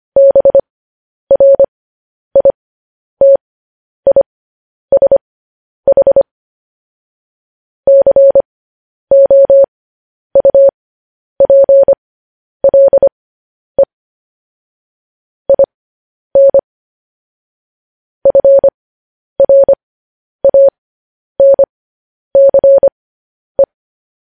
Morse code, heard as a keyed tone, about 25 wpm, spaced as for 10 wpm, 550 Hz. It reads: BRITISH COUPLE IN FRANCE
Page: News Headlines in Morse Code at 10 WPM